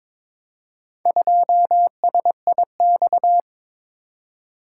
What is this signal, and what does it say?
Morse code, heard as a keyed tone, about 23 wpm, 700 Hz. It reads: 2SIX
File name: practice / 22wpm / Four Letter-Number Combinations - 22wpm